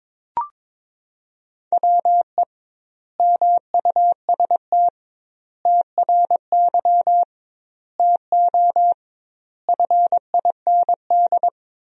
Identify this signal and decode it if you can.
Morse code, heard as a keyed tone, about 22 wpm, 700 Hz. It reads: WE MUST TRY TO FIND
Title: 2x Word-Spacing - Sentences from Top 400 Words - 22wpm